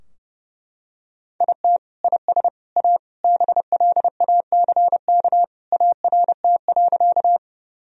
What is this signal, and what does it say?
Morse code, heard as a keyed tone, about 30 wpm, 700 Hz. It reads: IT IS A BLACK ART.
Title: Sentences from Top 1000 Words - Review Part 3 - 30wpm